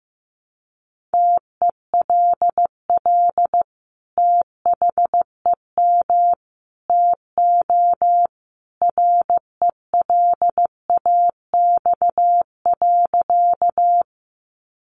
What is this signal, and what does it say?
Morse code, heard as a keyed tone, about 15 wpm, 700 Hz. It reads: TELL THEM TO RELAX.
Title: Sentences from Top 1400 Words 15wpm